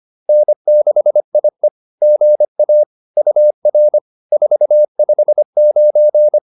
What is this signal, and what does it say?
Morse code, heard as a keyed tone, about 25 wpm, 600 Hz. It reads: N6IE GA UR 459